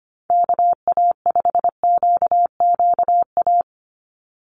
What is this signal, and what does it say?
Morse code, heard as a keyed tone, about 25 wpm, 700 Hz. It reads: KA5QQA
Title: NA SOTA Chaser Callsigns 25wpm